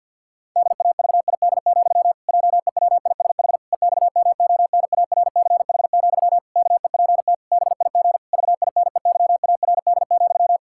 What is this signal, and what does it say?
Morse code, heard as a keyed tone, about 49 wpm, 700 Hz. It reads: DAVID, JEWISH EXMONARCH, KEPT BIG VINEYARD,
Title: Pangram Challenge v2 - 50wpm